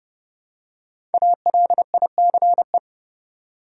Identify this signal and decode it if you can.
Morse code, heard as a keyed tone, about 30 wpm, 700 Hz. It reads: ALICE
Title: USA Common Names - Max 5 Letters - 30wpm